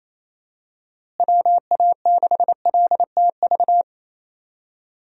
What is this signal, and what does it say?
Morse code, heard as a keyed tone, about 28 wpm, 700 Hz. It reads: WA6LTV
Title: USA Call Signs - 6 Characters - Repeated 3x - 28wpm